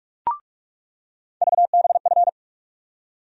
Morse code, heard as a keyed tone, about 42 wpm, 700 Hz. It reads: VBF
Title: USA Call Sign Postfixes - 3 Letters 45wpm